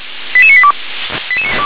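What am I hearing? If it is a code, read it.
Five-tone selective call (ZVEI), heard as quick runs of five tones, tones of about 70 ms. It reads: 88982
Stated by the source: ZVEI five-tone selective call 88841, received by SDR.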